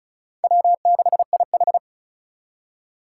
Morse code, heard as a keyed tone, about 36 wpm, 700 Hz. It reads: W6IH
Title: USA Call Signs - 4 Characters - 3x Character Spacing - 35wpm